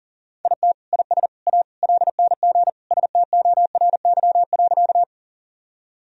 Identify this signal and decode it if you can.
Morse code, heard as a keyed tone, about 40 wpm, 700 Hz. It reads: IT IS A LNG STORY.
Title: Sentences from Top 300 Words and Abbreviations 40wpm